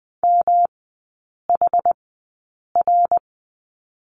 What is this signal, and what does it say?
Morse code, heard as a keyed tone, about 20 wpm, spaced as for 8 wpm, 700 Hz. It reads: MHR
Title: 107 Lesson 18e - Letter m - Three Characters 20wpm